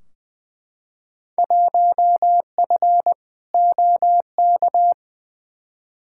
Morse code, heard as a keyed tone, about 20 wpm, 700 Hz. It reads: 1F OK
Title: Field Day Exchanges Encore - 20wpm